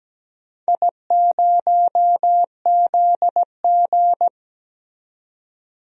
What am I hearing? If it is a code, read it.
Morse code, heard as a keyed tone, about 17 wpm, 700 Hz. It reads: I0ZG
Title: DXCC-100 Call Signs - 4 Characters - 17wpm